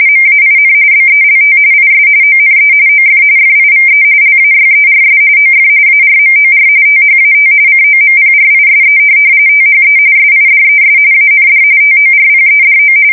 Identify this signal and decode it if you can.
RTTY, about 45.45 baud. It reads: MAIN BELT IS ONE OF FIVE LAYERS OF ARMOR THAT COMPRISE THE SIDES AND TORPEDO
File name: fsk170 / USS-New-Jersey